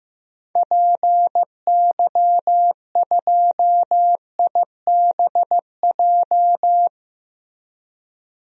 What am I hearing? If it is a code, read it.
Morse code, heard as a keyed tone, about 15 wpm, 700 Hz. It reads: PY2IBJ
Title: DXCC-100 Call Signs - 6 Characters - Repeated 3x - 15wpm